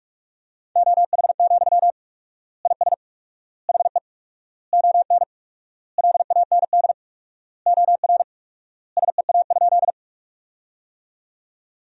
Morse code, heard as a keyed tone, about 46 wpm, 700 Hz. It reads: OH, IS HE ON LAND OR SEA?
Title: 4x Word-Spacing - Sentences from Top 500 Words - 45wpm